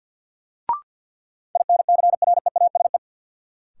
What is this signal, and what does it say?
Morse code, heard as a keyed tone, about 53 wpm, 700 Hz. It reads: INCREASE
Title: Sentences from Top 1000 Words - Review Part 4 - New Word Drill - 50wpm